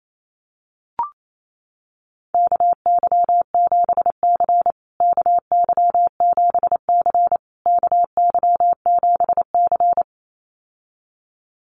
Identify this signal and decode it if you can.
Morse code, heard as a keyed tone, about 28 wpm, 700 Hz. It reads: KY7C KY7C KY7C
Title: USA Call Signs Repeated 3x 28wpm